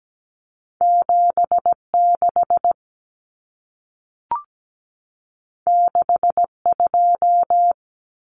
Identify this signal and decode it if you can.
Morse code, heard as a keyed tone, about 17 wpm, 700 Hz. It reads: 76 62
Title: Common Cut Numbers - 2-Digit 17wpm